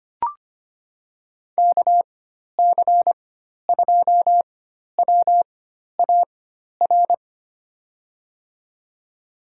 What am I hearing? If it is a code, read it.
Morse code, heard as a keyed tone, about 25 wpm, spaced as for 12 wpm, 700 Hz. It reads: KC2WAR
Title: USA Call Signs - 4x Character Spacing - 25wpm